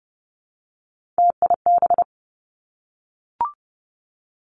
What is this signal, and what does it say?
Morse code, heard as a keyed tone, about 30 wpm, 700 Hz. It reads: TIB
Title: Three Challenging Letter-Number Combinations 30wpm